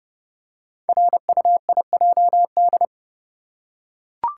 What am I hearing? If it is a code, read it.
Morse code, heard as a keyed tone, about 30 wpm, 700 Hz. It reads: RUIJD
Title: Code Groups - 5 Letters - 30wpm